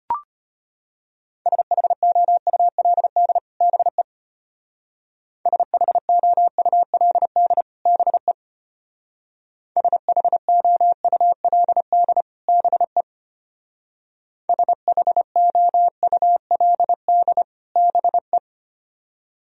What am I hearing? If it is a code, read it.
Morse code, heard as a keyed tone, about 29 wpm, 700 Hz. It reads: SHOULD BE SHOULD BE SHOULD BE SHOULD BE
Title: Speed-Racing - Sets of 2 Words 25wpm